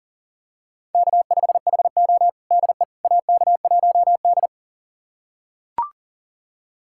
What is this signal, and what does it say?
Morse code, heard as a keyed tone, about 37 wpm, 700 Hz. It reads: K5HO DE AK1D
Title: USA Callsign DE Callsign - 4 Characters - 40wpm